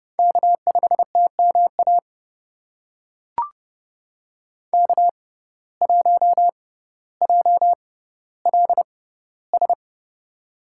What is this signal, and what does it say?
Morse code, heard as a keyed tone, about 30 wpm, spaced as for 15 wpm, 700 Hz. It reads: K5TMA K1JLS
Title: USA Call Signs - 5 Characters - 6x Character Spacing - 30wpm